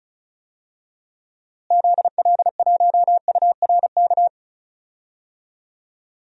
Morse code, heard as a keyed tone, about 34 wpm, 700 Hz. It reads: ZL1URK
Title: DXCC-100 Call Signs - 6 Characters - Repeated 3x - 35wpm